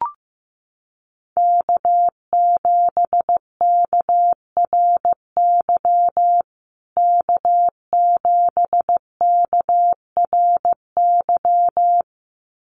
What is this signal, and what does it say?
Morse code, heard as a keyed tone, about 15 wpm, 700 Hz. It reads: K7KRY K7KRY
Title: USA Call Signs Repeated 2x 15wpm